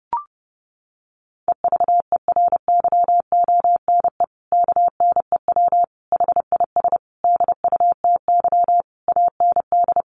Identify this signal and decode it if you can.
Morse code, heard as a keyed tone, about 30 wpm, 700 Hz. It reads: EVERYONE KNEW HIS DUTY AND
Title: Sentences from Top 1500 Words 30wpm